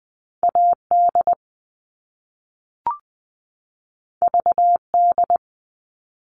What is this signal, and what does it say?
Morse code, heard as a keyed tone, about 20 wpm, 700 Hz. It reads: AD VD
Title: All Cut Numbers - 2-Digit 20wpm